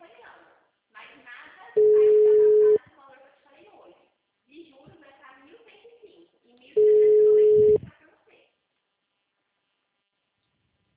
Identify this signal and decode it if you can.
Ringback tone (Europe, 425 Hz), with speech.